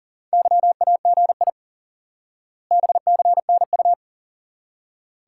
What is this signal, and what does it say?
Morse code, heard as a keyed tone, about 37 wpm, 700 Hz. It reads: YAGI BCNU
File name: IQE Course 3 - Lesson 89 - YAGI - Two Elements - 40wpm